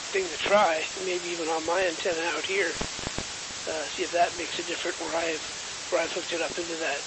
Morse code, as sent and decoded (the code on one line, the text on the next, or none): none